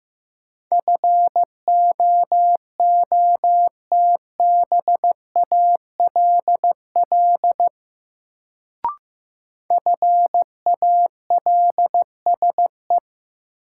Morse code, heard as a keyed tone, about 15 wpm, 700 Hz. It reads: FOOTBALL FALSE
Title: Sentences from Top 1600 Words - New Word Drill Alt - 15wpm